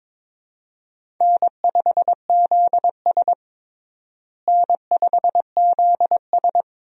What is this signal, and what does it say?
Morse code, heard as a keyed tone, about 22 wpm, 700 Hz. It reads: N5ZS N5ZS
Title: USA Call Signs - 4 Characters - Repeated 2x - 22wpm